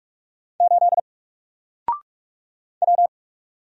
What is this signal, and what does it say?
Morse code, heard as a keyed tone, about 44 wpm, 700 Hz. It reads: 8 W
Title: Single Character (Letter:Number:Punctuation:Prosign) 45wpm